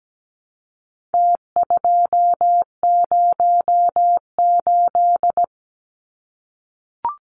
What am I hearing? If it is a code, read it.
Morse code, heard as a keyed tone, about 17 wpm, 700 Hz. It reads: T208
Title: Numbers - Contest Style - 4-Digit 17wpm